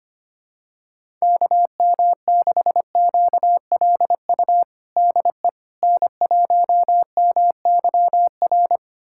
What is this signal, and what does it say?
Morse code, heard as a keyed tone, about 25 wpm, 700 Hz. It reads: KM6QLU DE N1MYR
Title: USA Callsign DE Callsign 25wpm